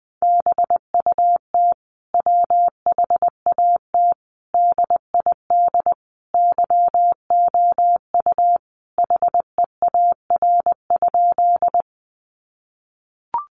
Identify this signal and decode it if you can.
Morse code, heard as a keyed tone, about 20 wpm, 700 Hz. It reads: BUT WHAT DID YOU HEAR?
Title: Sentences from Top 300 Words 20wpm